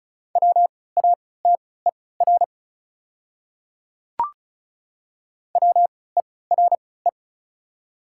Morse code, heard as a keyed tone, about 36 wpm, spaced as for 20 wpm, 700 Hz. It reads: WATER WERE
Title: 3x Character Spacing - Top 200 Words - 35wpm